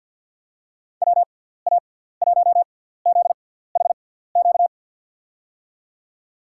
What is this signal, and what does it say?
Morse code, heard as a keyed tone, about 49 wpm, spaced as for 17 wpm, 700 Hz. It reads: WA1ZHQ
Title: USA Call Signs - 6 Characters - 6x Character Spacing - 50wpm